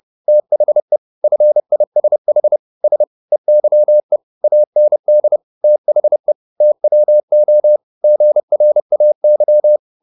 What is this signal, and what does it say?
Morse code, heard as a keyed tone, about 30 wpm, 600 Hz. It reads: THE FISH S EYE AND THE TWO GRAY